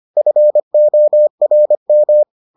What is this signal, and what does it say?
Morse code, heard as a keyed tone, about 25 wpm, 600 Hz. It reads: FORM